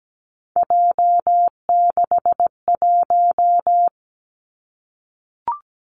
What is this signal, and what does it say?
Morse code, heard as a keyed tone, about 17 wpm, 700 Hz. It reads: J61
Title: Three Challenging Letter-Number Combinations 17wpm